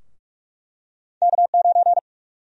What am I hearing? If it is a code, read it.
Morse code, heard as a keyed tone, about 45 wpm, 700 Hz. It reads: K9
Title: USA Call Sign Prefixes - Repeated 2x - 45wpm